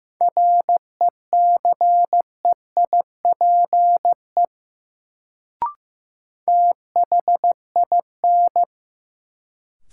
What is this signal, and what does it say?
Morse code, heard as a keyed tone, about 15 wpm, 700 Hz. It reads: RECEIPE THIN
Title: Sentences from Top 1200 Words - New Word Drill - 15wpm